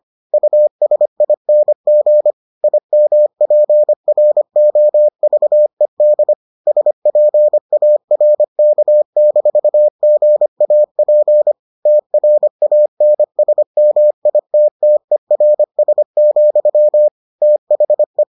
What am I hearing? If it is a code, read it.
Morse code, heard as a keyed tone, about 25 wpm, 600 Hz. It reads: USING IMPROVED SPARK-GAP TRANSMITTERS, THE